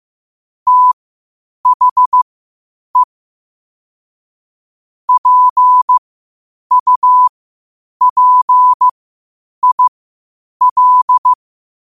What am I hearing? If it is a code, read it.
Morse code, heard as a keyed tone, about 15 wpm, spaced as for 8 wpm, 1000 Hz. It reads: THE PUPIL
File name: qotdfiles / 8wpm